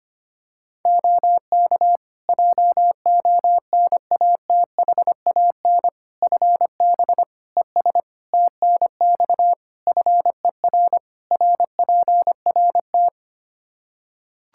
Morse code, heard as a keyed tone, about 25 wpm, 700 Hz. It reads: OK JONATHAN FB ES TNX FER RPRT
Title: Practice QSO - Exchange 2 - Info - 25wpm